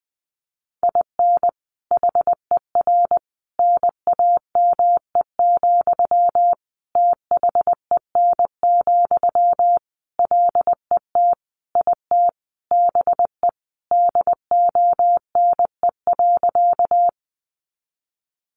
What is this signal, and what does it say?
Morse code, heard as a keyed tone, about 20 wpm, 700 Hz. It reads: IN HER NAME, THEN, LET IT BE DONE.
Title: Sentences from Top 500 Words 20wpm